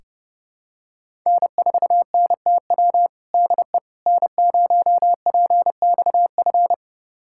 Morse code, heard as a keyed tone, about 32 wpm, 700 Hz. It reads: N4NTW DE N0PXF
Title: USA Callsign DE Callsign - 5 Characters - 30wpm